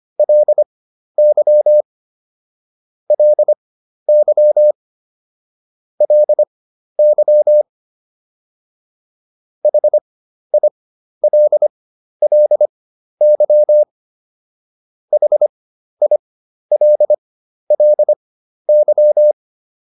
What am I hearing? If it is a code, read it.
Morse code, heard as a keyed tone, about 25 wpm, spaced as for 12 wpm, 600 Hz. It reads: LY LY LY HILLY HILLY